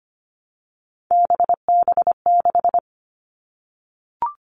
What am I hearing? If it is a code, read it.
Morse code, heard as a keyed tone, about 25 wpm, 700 Hz. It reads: BB6
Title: Three Challenging Letter-Number Combinations 25wpm